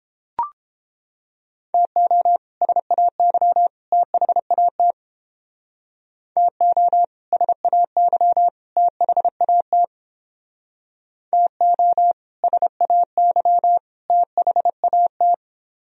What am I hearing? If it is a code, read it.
Morse code, heard as a keyed tone, about 30 wpm, 700 Hz. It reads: TO SAY THAT TO SAY THAT TO SAY THAT